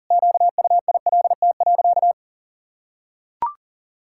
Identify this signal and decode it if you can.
Morse code, heard as a keyed tone, about 40 wpm, 700 Hz. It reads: QUILT.
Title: Pangram Challenge v2 - 40wpm